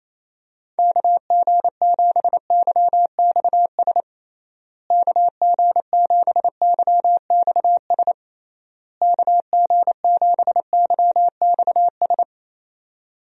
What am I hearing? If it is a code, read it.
Morse code, heard as a keyed tone, about 28 wpm, 700 Hz. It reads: KG7YXS KG7YXS KG7YXS
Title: USA Call Signs - 6 Characters - Repeated 3x - 28wpm